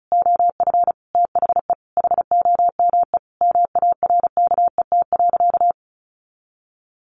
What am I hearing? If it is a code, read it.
Morse code, heard as a keyed tone, about 34 wpm, 700 Hz. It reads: OF THE HOME MARKET.
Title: Sentences from Top 1400 Words 35wpm